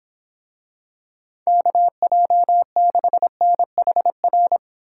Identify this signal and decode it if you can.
Morse code, heard as a keyed tone, about 26 wpm, 700 Hz. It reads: KJ6NHR